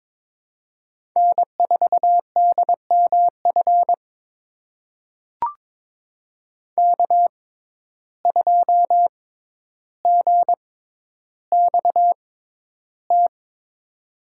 Morse code, heard as a keyed tone, about 22 wpm, spaced as for 11 wpm, 700 Hz. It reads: N4DMF K2GXT
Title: USA Call Signs - 6x Character Spacing - 22wpm